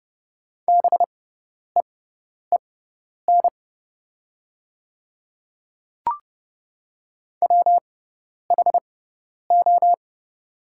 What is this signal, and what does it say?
Morse code, heard as a keyed tone, about 30 wpm, spaced as for 10 wpm, 700 Hz. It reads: BEEN WHO
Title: 6x Character Spacing - Top 100 Words - 30wpm